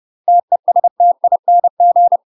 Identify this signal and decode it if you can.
Morse code, heard as a keyed tone, about 30 wpm, 700 Hz. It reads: TESTING
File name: UK MASTER QSOs TRAINING/30WPM